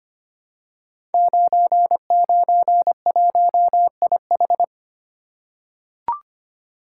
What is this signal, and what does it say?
Morse code, heard as a keyed tone, about 25 wpm, 700 Hz. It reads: 991IH
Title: Code Groups - 5 Letter:Numbers - 25wpm